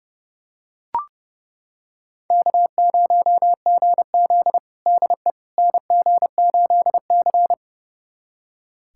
Morse code, heard as a keyed tone, about 30 wpm, 700 Hz. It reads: K0GZ DE NG8C
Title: USA Callsign DE Callsign - 4 Characters - 30wpm